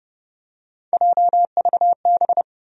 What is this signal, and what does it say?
Morse code, heard as a keyed tone, about 30 wpm, 700 Hz. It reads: JVB
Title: USA Call Sign Postfixes - 3 Letters 30wpm